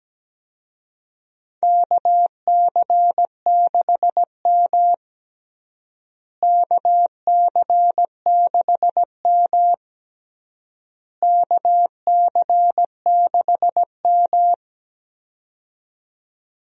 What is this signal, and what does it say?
Morse code, heard as a keyed tone, about 17 wpm, 700 Hz. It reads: KC6M KC6M KC6M
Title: USA Call Signs - 4 Characters - Repeated 3x - 17wpm